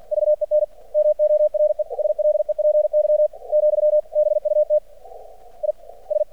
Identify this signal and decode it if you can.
Morse code, heard as a keyed tone, about 50 wpm, 600 Hz. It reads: VEN MORE SLEPY , BUT I S
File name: archived / CRSnet